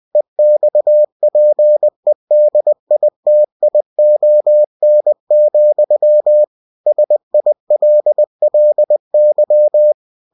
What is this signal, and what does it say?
Morse code, heard as a keyed tone, about 20 wpm, 600 Hz. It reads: EXPEDITION, SILLY